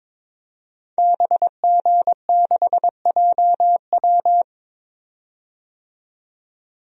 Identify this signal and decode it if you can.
Morse code, heard as a keyed tone, about 22 wpm, 700 Hz. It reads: BG6JW
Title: DXCC-100 Call Signs - 5 Characters - 22wpm